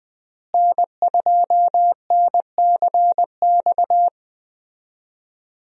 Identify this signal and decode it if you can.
Morse code, heard as a keyed tone, about 20 wpm, 700 Hz. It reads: N2NCX
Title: USA Call Signs - 5 Characters - Repeated 2x - 20wpm